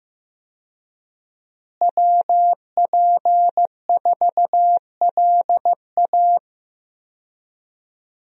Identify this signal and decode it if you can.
Morse code, heard as a keyed tone, about 15 wpm, 700 Hz. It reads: WP4LA
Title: USA Call Signs - 5 Characters - Repeated 3x - 15wpm